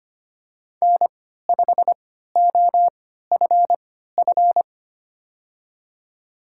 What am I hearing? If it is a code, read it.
Morse code, heard as a keyed tone, about 25 wpm, spaced as for 14 wpm, 700 Hz. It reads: N5OFF
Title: USA Call Signs - 5 Characters - 3x Character Spacing - 25wpm